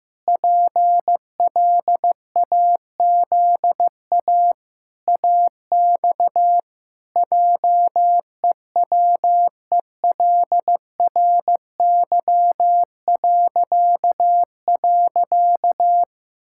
Morse code, heard as a keyed tone, about 15 wpm, 700 Hz. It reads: PLAZA AX JEWELRY..
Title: Pangram Challenge v2 - 15wpm